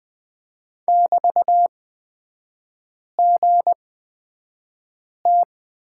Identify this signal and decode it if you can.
Morse code, heard as a keyed tone, about 20 wpm, spaced as for 5 wpm, 700 Hz. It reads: <BT>GT
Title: 224 Lesson 36e - BT - Three Characters 20wpm